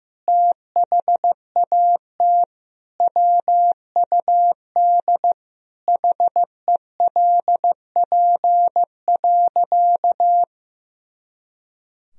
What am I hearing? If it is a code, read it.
Morse code, heard as a keyed tone, about 15 wpm, 700 Hz. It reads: THAT WUD HELP.
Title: Sentences from Top 200 Words and Abbreviations 15wpm